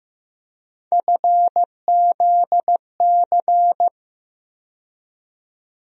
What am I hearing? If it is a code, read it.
Morse code, heard as a keyed tone, about 15 wpm, 700 Hz. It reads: FZC
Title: USA Call Sign Postfixes - 3 Letters 15wpm